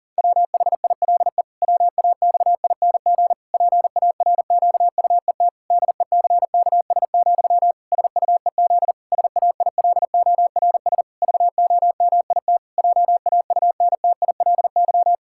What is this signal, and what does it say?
Morse code, heard as a keyed tone, about 37 wpm, 700 Hz. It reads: WHILE WAXING PARQUET DECKS, SUEZ SAILORS VOMIT JAUNTILY
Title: Pangram Challenge v2 - 40wpm